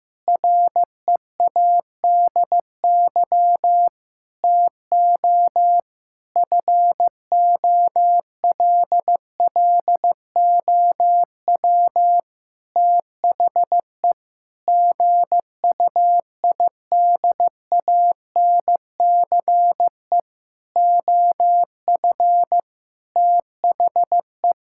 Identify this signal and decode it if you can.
Morse code, heard as a keyed tone, about 15 wpm, 700 Hz. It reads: READY TO FOLLOW THE GUIDANCE OF THE
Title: Sentences from Top 1600 Words 15wpm